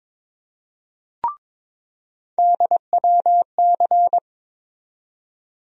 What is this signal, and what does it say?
Morse code, heard as a keyed tone, about 22 wpm, 700 Hz. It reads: DWC
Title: USA Call Sign Postfixes - 3 Letters 22wpm